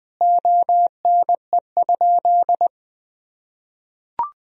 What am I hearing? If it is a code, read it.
Morse code, heard as a keyed tone, about 20 wpm, 700 Hz. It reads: ONE?
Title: Sentences from Top 1000 Words - Review Part 2 - 20wpm